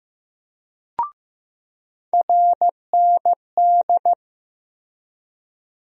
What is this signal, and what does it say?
Morse code, heard as a keyed tone, about 15 wpm, 700 Hz. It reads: RND
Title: USA Call Sign Postfixes - 3 Letters 15wpm